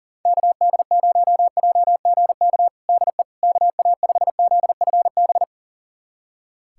Morse code, heard as a keyed tone, about 40 wpm, 700 Hz. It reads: KD0JGK DE KA5ZFB
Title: USA Callsign DE Callsign - 6 Characters - 40wpm